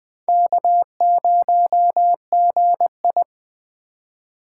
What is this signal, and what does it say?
Morse code, heard as a keyed tone, about 24 wpm, 700 Hz. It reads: K0GI
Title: Speed Racing - USA 1x2 Call Signs 20wpm